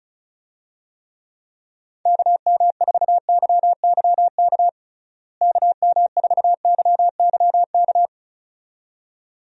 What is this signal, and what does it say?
Morse code, heard as a keyed tone, about 34 wpm, 700 Hz. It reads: KM4YYK KM4YYK
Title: USA Call Signs - 6 Characters - Repeated 2x - 35wpm